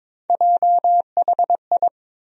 Morse code, heard as a keyed tone, about 22 wpm, 700 Hz. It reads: JHI
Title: USA Call Sign Postfixes - 3 Letters 22wpm